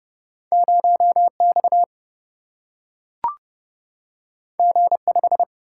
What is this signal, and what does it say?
Morse code, heard as a keyed tone, about 30 wpm, 700 Hz. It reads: 0X G5
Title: Single-Letter Single-Number 30wpm